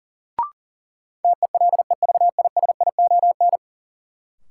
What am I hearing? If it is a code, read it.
Morse code, heard as a keyed tone, about 40 wpm, 700 Hz. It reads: TELEVISION
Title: Sentences from Top 1000 Words - Review Part 4 - New Word Drill - 40wpm